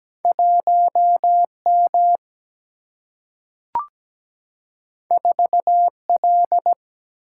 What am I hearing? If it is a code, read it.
Morse code, heard as a keyed tone, about 17 wpm, 700 Hz. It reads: 1 M 4 L
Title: Single-Letter Single-Number 17wpm